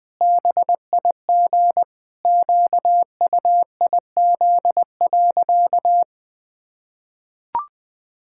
Morse code, heard as a keyed tone, about 20 wpm, 700 Hz. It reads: BIG QUIZ.
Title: Pangram Challenge v2 - 20wpm